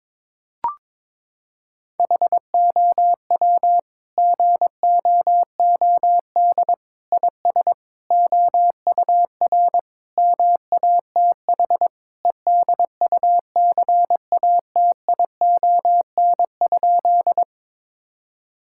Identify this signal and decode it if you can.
Morse code, heard as a keyed tone, about 22 wpm, 700 Hz. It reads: HOW GOOD IS OUR MATH EDUCATION?
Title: Sentences from Top 1000 Words - Review Part 6 - 22wpm